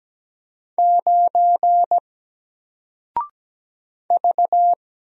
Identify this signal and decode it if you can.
Morse code, heard as a keyed tone, about 17 wpm, 700 Hz. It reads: 9 V
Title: Single Letter-Number Alt 17wpm